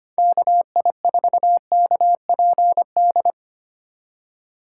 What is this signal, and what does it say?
Morse code, heard as a keyed tone, about 25 wpm, 700 Hz. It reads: KI4KPD
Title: USA Call Signs - 6 Characters - 2x Character Spacing - 25wpm